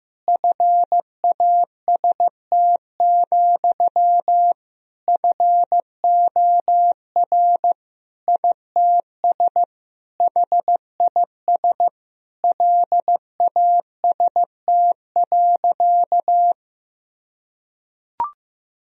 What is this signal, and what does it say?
Morse code, heard as a keyed tone, about 15 wpm, 700 Hz. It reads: FAST, FOR ITS HIS LAST.
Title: Sentences from Top 1000 Words - Review Part 2 - 15wpm